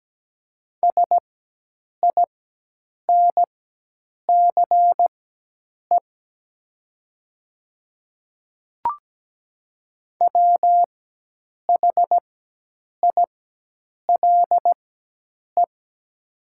Morse code, heard as a keyed tone, about 17 wpm, spaced as for 8 wpm, 700 Hz. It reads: SINCE WHILE
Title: 4x Character Spacing - Top 500 Words - 17wpm